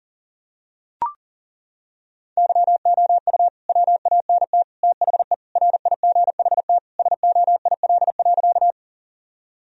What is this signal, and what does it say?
Morse code, heard as a keyed tone, about 40 wpm, 700 Hz. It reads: YOU WANT THE RIGHT SOIL.
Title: Sentences from Top 1000 Words - Review Part 4 - 40wpm